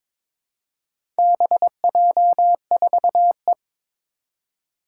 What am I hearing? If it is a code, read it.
Morse code, heard as a keyed tone, about 22 wpm, 700 Hz. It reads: BJ4E
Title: Four Challenging Letter-Number Combinations 22wpm